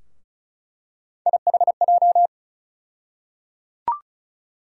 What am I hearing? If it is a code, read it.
Morse code, heard as a keyed tone, about 36 wpm, 700 Hz. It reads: IHJ
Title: USA Call Sign Postfixes - 3 Letters 35wpm